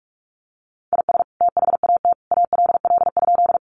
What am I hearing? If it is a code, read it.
Morse code, heard as a keyed tone, about 44 wpm, 700 Hz. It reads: IS THAT ALL?
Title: Sentences from Top 100 Words 45wpm